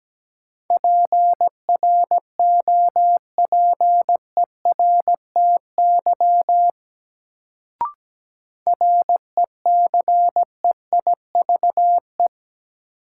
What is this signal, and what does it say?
Morse code, heard as a keyed tone, about 17 wpm, 700 Hz. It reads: PROPERTY RECEIVE
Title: Sentences from Top 1000 Words - New Word Drill - 17wpm